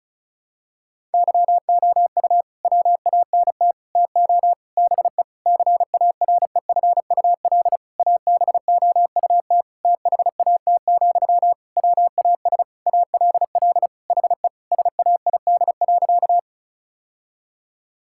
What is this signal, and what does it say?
Morse code, heard as a keyed tone, about 36 wpm, 700 Hz. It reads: YOU WANT TO BE CAREFUL ABOUT THAT, WAS ALL HE SAID.
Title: Sentences from Top 1600 Words 35wpm